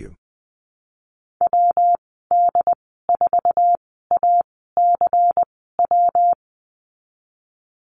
Morse code, heard as a keyed tone, about 20 wpm, spaced as for 15 wpm, 700 Hz. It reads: WD4ACW
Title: 186 Lesson 30g - Letter b - Call Signs 20wpm